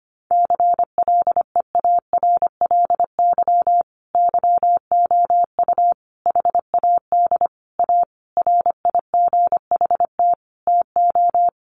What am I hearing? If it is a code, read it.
Morse code, heard as a keyed tone, about 25 wpm, 700 Hz. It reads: CLEARLY YOU HAD A RIGHT TO
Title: Sentences from Top 1300 Words 25wpm